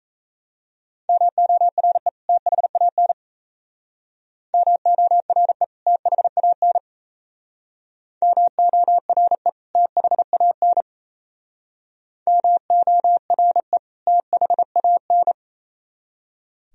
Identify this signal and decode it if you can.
Morse code, heard as a keyed tone, about 35 wpm, 700 Hz. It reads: MORE THAN MORE THAN MORE THAN MORE THAN